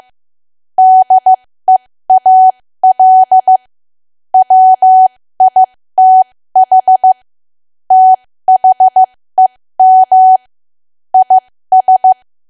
Morse code, heard as a keyed tone, about 15 wpm, spaced as for 13 wpm, 750 Hz. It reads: DEAL WITH THEM IS
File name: cwa / W1AW-Level2 / 13WPM